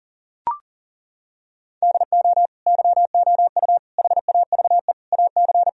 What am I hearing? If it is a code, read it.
Morse code, heard as a keyed tone, about 40 wpm, 700 Hz. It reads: DO YOU HAVE AC
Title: Sentences from Top 1000 Words - Review Part 4 - 40wpm